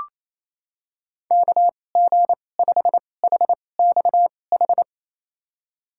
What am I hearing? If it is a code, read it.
Morse code, heard as a keyed tone, about 28 wpm, spaced as for 20 wpm, 700 Hz. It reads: KG5HXH
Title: USA Call Signs - 6 Characters - 2x Character Spacing - 28wpm